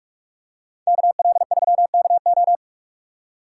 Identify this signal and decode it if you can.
Morse code, heard as a keyed tone, about 44 wpm, 700 Hz. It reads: KL3KO